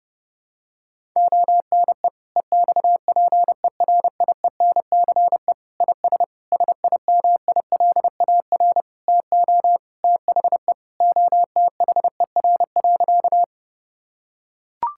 Morse code, heard as a keyed tone, about 30 wpm, 700 Hz. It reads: ONE EXPERIENCE IS SIMILAR TO THE OTHER.
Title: Sentences from Top 1000 Words - Review Part 5 - 30wpm